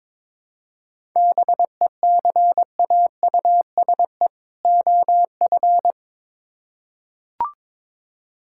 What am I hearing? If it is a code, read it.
Morse code, heard as a keyed tone, about 22 wpm, 700 Hz. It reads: BECAUSE OF
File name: Build A Phrase - 2-Word Phrases - 22wpm